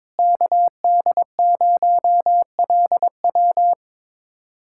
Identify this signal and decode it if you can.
Morse code, heard as a keyed tone, about 22 wpm, 700 Hz. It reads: KD0LW
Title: USA Callsign DE Callsign - 5 Characters - 22wpm